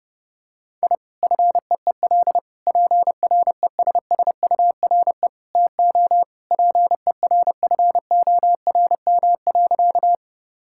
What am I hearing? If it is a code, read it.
Morse code, heard as a keyed tone, about 30 wpm, 700 Hz. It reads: I FEEL PRESSURE TO PERFORM.
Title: Sentences from Top 1000 Words - Review Part 6 - 30wpm